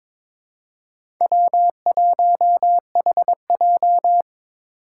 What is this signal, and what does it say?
Morse code, heard as a keyed tone, about 26 wpm, 700 Hz. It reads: W1HJ
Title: Speed Racing - USA 1x2 Call Signs 22wpm